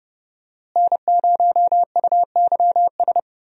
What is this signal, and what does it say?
Morse code, heard as a keyed tone, about 30 wpm, 700 Hz. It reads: N0UYS
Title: USA Call Signs - 5 Characters - Repeated 3x - 30wpm